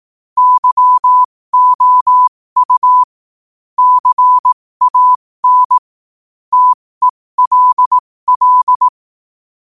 Morse code, heard as a keyed tone, about 18 wpm, spaced as for 15 wpm, 1000 Hz. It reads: YOU CAN TELL
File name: qotdfiles / 15wpm